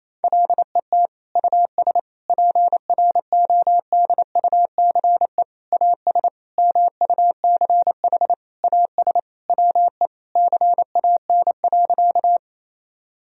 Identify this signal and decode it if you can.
Morse code, heard as a keyed tone, about 28 wpm, 700 Hz. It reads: LET US PRODUCE AS MUCH AS WE CAN.
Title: Sentences from Top 1000 Words - Review Part 3 - 28wpm